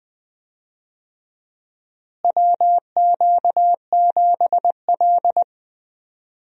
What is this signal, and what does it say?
Morse code, heard as a keyed tone, about 20 wpm, 700 Hz. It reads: WQ7L
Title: USA Call Signs - 4 Characters - Repeated 2x - 20wpm